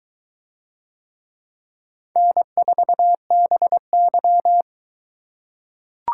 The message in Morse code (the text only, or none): N4BY